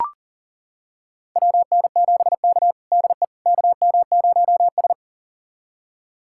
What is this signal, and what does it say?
Morse code, heard as a keyed tone, about 37 wpm, 700 Hz. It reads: WN7K DE KM0S
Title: USA Callsign DE Callsign - 4 Characters - 40wpm